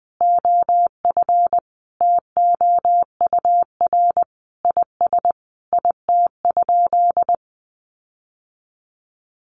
Morse code, heard as a keyed tone, about 20 wpm, 700 Hz. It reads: OF TOUR IS IT?
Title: Sentences from Top 1400 Words 20wpm